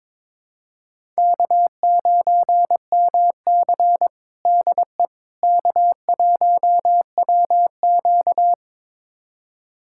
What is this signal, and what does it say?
Morse code, heard as a keyed tone, about 22 wpm, 700 Hz. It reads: K9MC DE K1WQ
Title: USA Callsign DE Callsign - 4 Characters - 22wpm